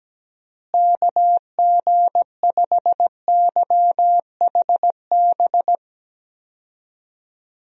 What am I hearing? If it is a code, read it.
Morse code, heard as a keyed tone, about 17 wpm, 700 Hz. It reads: KG5YHB
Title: USA Call Signs - 6 Characters - Repeated 2x - 17wpm